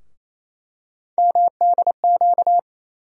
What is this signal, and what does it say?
Morse code, heard as a keyed tone, about 28 wpm, 700 Hz. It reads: MDQ
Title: USA Call Sign Postfixes - 3 Letters 28wpm